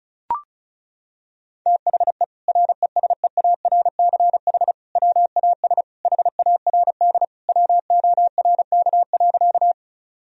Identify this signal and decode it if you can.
Morse code, heard as a keyed tone, about 34 wpm, 700 Hz. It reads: THE RESEARCH WAS HARD WORK.
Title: Sentences from Top 1000 Words - Review Part 4 - 35wpm